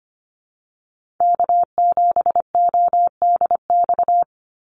Morse code, heard as a keyed tone, about 25 wpm, 700 Hz. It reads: K7ODX
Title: NA SOTA Activator Callsigns 25wpm